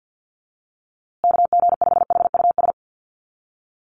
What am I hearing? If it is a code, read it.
Morse code, heard as a keyed tone, about 50 wpm, 700 Hz. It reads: KG5HUS
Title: USA Call Signs Repeated 3x 50wpm